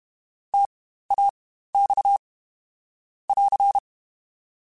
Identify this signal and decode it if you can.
Morse code, heard as a keyed tone, about 32 wpm, spaced as for 15 wpm, 800 Hz. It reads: TAX <AR>
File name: Spaced Alphas / Spaced Alphas mp3s / Ts sp @30